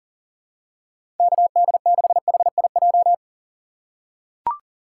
Morse code, heard as a keyed tone, about 40 wpm, 700 Hz. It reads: KD6HIJ
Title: USA Call Signs - 6 Characters - 3x Character Spacing - 40wpm